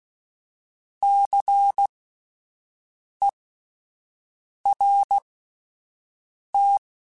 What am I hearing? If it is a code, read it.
Morse code, heard as a keyed tone, about 16 wpm, spaced as for 5 wpm, 800 Hz. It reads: CERT
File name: Spaced Alphas / Spaced Alphas mp3s / Cs sp @10